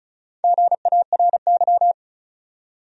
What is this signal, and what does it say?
Morse code, heard as a keyed tone, about 36 wpm, 700 Hz. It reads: GARY
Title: USA Common Names - 35wpm